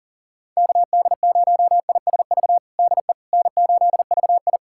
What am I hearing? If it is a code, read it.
Morse code, heard as a keyed tone, about 37 wpm, 700 Hz. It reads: KD0ISV DE N8VI
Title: USA Callsign DE Callsign 40wpm